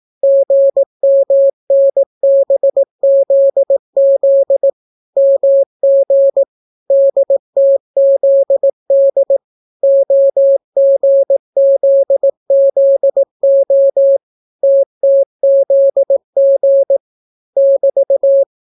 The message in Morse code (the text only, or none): GMNBZZ MG DTZD OGZZO TTZG <BT>